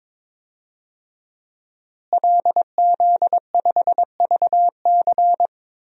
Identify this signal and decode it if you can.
Morse code, heard as a keyed tone, about 22 wpm, 700 Hz. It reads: LZ5VC
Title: DXCC-100 Call Signs - 5 Characters - Repeated 3x - 22wpm